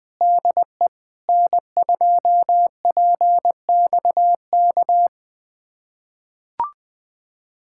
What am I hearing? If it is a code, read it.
Morse code, heard as a keyed tone, about 21 wpm, 700 Hz. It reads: DE N2PXK
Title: USA Callsign DE Callsign - 5 Characters - 20wpm